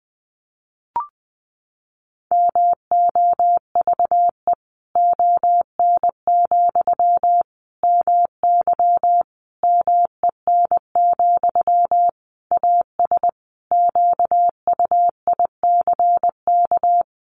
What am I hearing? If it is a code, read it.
Morse code, heard as a keyed tone, about 20 wpm, 700 Hz. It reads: MOVE ON, MY MEN, AS QUICK
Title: Sentences from Top 500 Words 20wpm